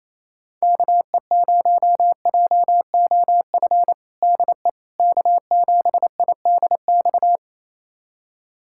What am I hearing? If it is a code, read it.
Morse code, heard as a keyed tone, about 28 wpm, 700 Hz. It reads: KE0JOF DE K7IDX
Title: USA Callsign DE Callsign 28wpm